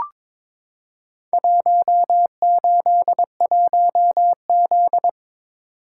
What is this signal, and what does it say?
Morse code, heard as a keyed tone, about 22 wpm, 700 Hz. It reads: 181Z
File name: practice / 22wpm / Four Letter-Number Combinations - 22wpm